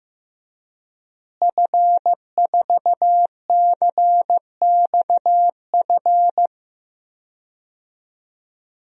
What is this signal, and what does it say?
Morse code, heard as a keyed tone, about 15 wpm, 700 Hz. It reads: F4CXF
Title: DXCC-100 Call Signs - 5 Characters - Repeated 3x - 15wpm